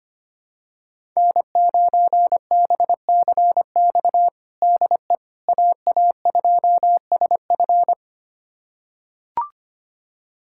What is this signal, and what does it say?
Morse code, heard as a keyed tone, about 25 wpm, 700 Hz. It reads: N9BCX DE AA2SF
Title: USA Callsign DE Callsign - 5 Characters - 25wpm